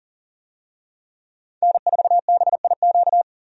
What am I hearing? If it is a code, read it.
Morse code, heard as a keyed tone, about 38 wpm, 700 Hz. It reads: N4BIQ
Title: USA Call Signs - 5 Characters - 40wpm